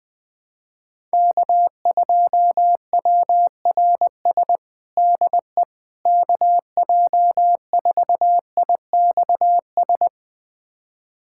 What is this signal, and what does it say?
Morse code, heard as a keyed tone, about 20 wpm, 700 Hz. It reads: K2WRS DE KJ4IXS
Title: USA Callsign DE Callsign 20wpm